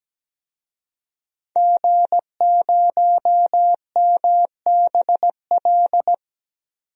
Morse code, heard as a keyed tone, about 17 wpm, 700 Hz. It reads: G0MBL
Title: DXCC-100 Call Signs - 5 Characters - 17wpm